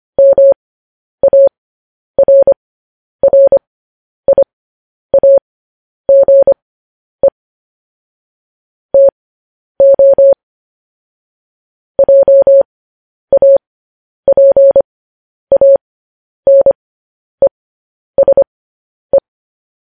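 Morse code, heard as a keyed tone, about 25 wpm, spaced as for 10 wpm, 550 Hz. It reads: MARRIAGE TO JAPANESE
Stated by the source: Morse news headlines at 10.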